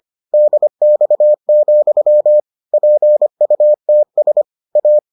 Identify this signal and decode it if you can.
Morse code, heard as a keyed tone, about 25 wpm, 600 Hz. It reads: DX, PUTS A